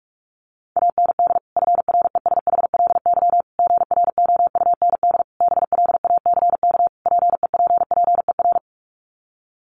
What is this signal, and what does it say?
Morse code, heard as a keyed tone, about 44 wpm, 700 Hz. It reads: AND FRESHLY GROUND BLACK PEPPER
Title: Sets of 5 Words Encore 45wpm